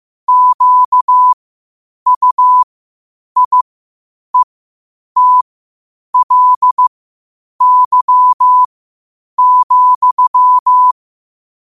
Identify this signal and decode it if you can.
Morse code, heard as a keyed tone, about 15 wpm, spaced as for 8 wpm, 1000 Hz. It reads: QUIETLY,
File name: qotdfiles / 8wpm